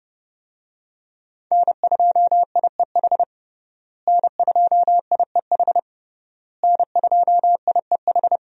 Morse code, heard as a keyed tone, about 32 wpm, 700 Hz. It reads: N2IEH N2IEH N2IEH
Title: USA Call Signs - 5 Characters - Repeated 3x - 30wpm